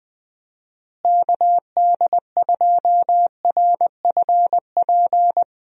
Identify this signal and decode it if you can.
Morse code, heard as a keyed tone, about 21 wpm, 700 Hz. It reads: KD2RFP
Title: USA Call Signs - 6 Characters - Repeated 3x - 20wpm